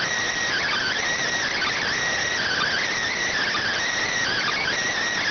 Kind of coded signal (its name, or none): Emergency Alert System (SAME) tones